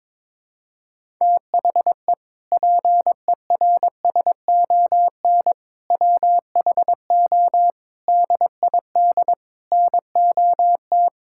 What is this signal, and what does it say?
Morse code, heard as a keyed tone, about 22 wpm, 700 Hz. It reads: THE PERSON WHO DID NOT
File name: Sentences with 7-Word Phrases - 22wpm